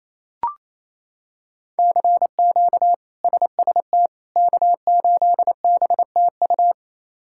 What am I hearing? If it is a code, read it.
Morse code, heard as a keyed tone, about 28 wpm, 700 Hz. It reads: CQ SST K8BTU
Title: K1USN SST QSO Practice - 28wpm